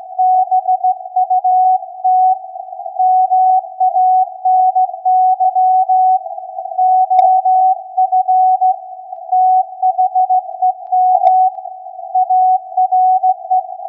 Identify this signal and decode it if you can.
Morse code, heard as a keyed tone, about 16 wpm, 740 Hz. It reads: BUT MANY OF THEM ARE
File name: iz7slz / VLF / SAQ 16-11-2022